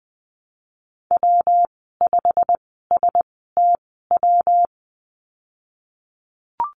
Morse code, heard as a keyed tone, about 20 wpm, spaced as for 15 wpm, 700 Hz. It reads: W5STW
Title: 161 Lesson 26g - Letter g - Call Signs 20wpm